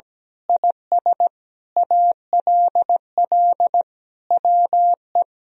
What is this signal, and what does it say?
Morse code, heard as a keyed tone, about 17 wpm, 700 Hz. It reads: IS ALL WE
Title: Sentences from Top 1000 Words - Review Part 2 - 17wpm